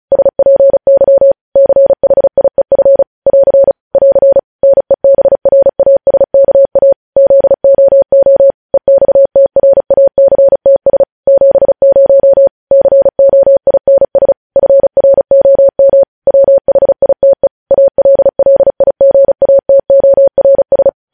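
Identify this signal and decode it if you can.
Morse code, heard as a keyed tone, about 36 wpm, 550 Hz. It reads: SPY CHIEF <AR> <AR> NEBRASKA ZOO EXTRACTS 70 COINS FROM WHITE ALLIGATORS
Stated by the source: Morse code news headlines sent at 35 wpm.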